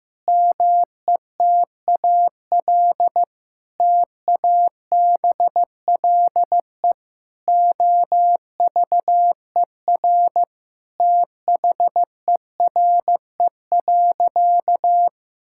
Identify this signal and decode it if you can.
Morse code, heard as a keyed tone, about 15 wpm, 700 Hz. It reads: METAL TABLE OVER THERE.
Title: Sentences from Top 1000 Words - Review Part 6 - 15wpm